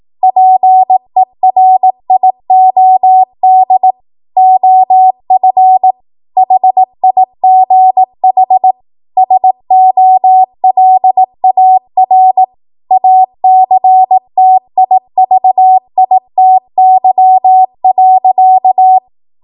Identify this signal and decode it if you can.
Morse code, heard as a keyed tone, about 18 wpm, 750 Hz. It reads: PERIOD OF HIGH SOLAR ACTIVITY.
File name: Morse / 18WPM